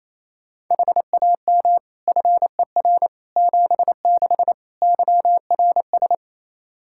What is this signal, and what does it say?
Morse code, heard as a keyed tone, about 28 wpm, 700 Hz. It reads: HAM FER 76 YRS
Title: Practice QSO - Exchange 3 - More Info - 28wpm